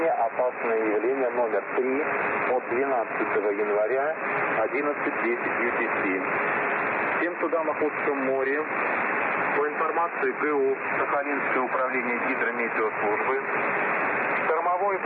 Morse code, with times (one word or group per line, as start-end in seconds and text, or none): none